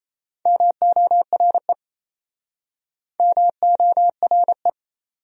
Morse code, heard as a keyed tone, about 34 wpm, 700 Hz. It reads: MORE MORE
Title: Speed-Racing Top 100 Words 28wpm